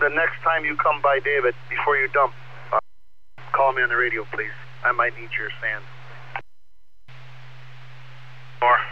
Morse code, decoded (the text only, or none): none